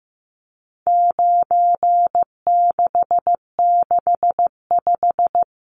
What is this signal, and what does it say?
Morse code, heard as a keyed tone, about 15 wpm, 700 Hz. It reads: 9665
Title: Numbers - 4-Digit 15wpm